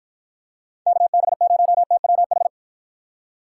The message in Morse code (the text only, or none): KB0TWH